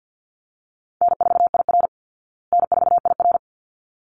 Morse code, heard as a keyed tone, about 50 wpm, 700 Hz. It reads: N4IR N4IR
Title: Easy Call Signs Repeated 2x - 50wpm